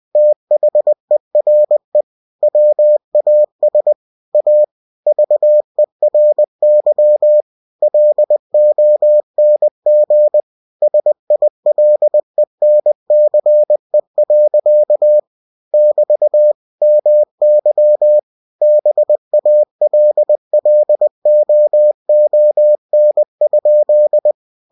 Morse code, heard as a keyed tone, about 20 wpm, 600 Hz. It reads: THERE WAS A VERY LONG SILENCE. <BT> MY BALLOON?
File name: cw-books / winnie-the-pooh / winnie-20wpm-page-0056